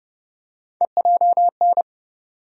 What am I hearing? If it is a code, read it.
Morse code, heard as a keyed tone, about 30 wpm, 700 Hz. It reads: EJN
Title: USA Call Sign Postfixes - 3 Letters 30wpm